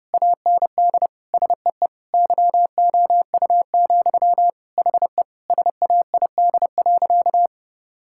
Morse code, heard as a keyed tone, about 30 wpm, 700 Hz. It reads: AND SEE YOU, HE SAID.
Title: Sentences from Top 1000 Words - Review Part 6 - 30wpm